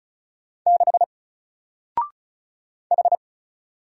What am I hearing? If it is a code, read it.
Morse code, heard as a keyed tone, about 35 wpm, 700 Hz. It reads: 6 H
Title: Single Letter-Number Alt 35wpm